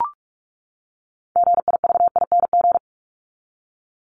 Morse code, heard as a keyed tone, about 44 wpm, 700 Hz. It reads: GIVING
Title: ING Suffix - Top 100 Words with Base Words 45wpm